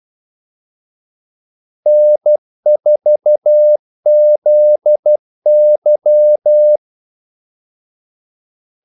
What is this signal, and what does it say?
Morse code, heard as a keyed tone, about 12 wpm, 600 Hz. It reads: N4ZY
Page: 2 ekstra ordmellemrum mellem kaldesignalerne